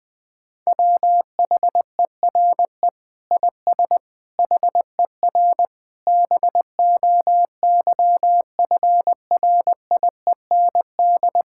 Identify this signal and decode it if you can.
Morse code, heard as a keyed tone, about 20 wpm, 700 Hz. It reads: WHERE IS HER BOYFRIEND
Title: Sentences from Top 1000 Words - Review Part 5 - 20wpm